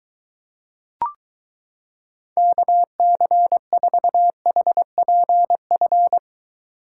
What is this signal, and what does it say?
Morse code, heard as a keyed tone, about 23 wpm, 700 Hz. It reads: KC4HPF